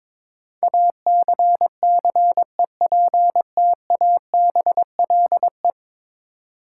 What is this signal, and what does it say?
Morse code, heard as a keyed tone, about 22 wpm, 700 Hz. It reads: ACCEPTABLE
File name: practice / 22wpm / 10 Letter Words - First 3 Letters Given - 22wpm